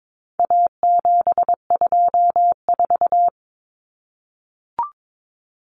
Morse code, heard as a keyed tone, about 22 wpm, 700 Hz. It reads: A724
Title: Common Cut Numbers - 4-Digit 22wpm